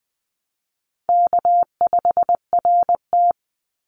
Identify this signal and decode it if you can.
Morse code, heard as a keyed tone, about 20 wpm, 700 Hz. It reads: K5RT
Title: Call Signs - 20wpm